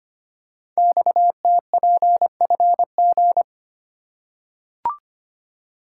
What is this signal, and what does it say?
Morse code, heard as a keyed tone, about 25 wpm, 700 Hz. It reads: XTPFG
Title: Code Groups - 5 Letters - 25wpm